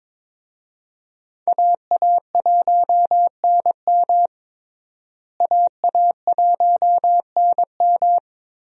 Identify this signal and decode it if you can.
Morse code, heard as a keyed tone, about 22 wpm, 700 Hz. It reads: AA1NM AA1NM
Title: USA Call Signs - 5 Characters - Repeated 3x - 22wpm